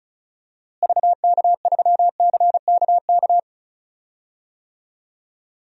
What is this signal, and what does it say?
Morse code, heard as a keyed tone, about 36 wpm, 700 Hz. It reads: VK3CKK
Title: DXCC-100 Call Signs - 6 Characters - Repeated 3x - 35wpm